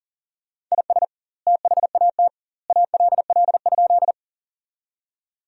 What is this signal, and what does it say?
Morse code, heard as a keyed tone, about 40 wpm, 700 Hz. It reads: IS THAT ALL?
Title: 2x Word-Spacing - Sentences from Top 100 Words - 40wpm